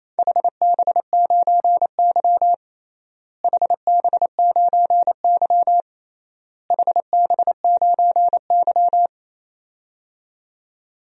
Morse code, heard as a keyed tone, about 27 wpm, 700 Hz. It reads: HB9Y HB9Y HB9Y
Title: DXCC-100 Call Signs - 4 Characters - Repeated 3x - 28wpm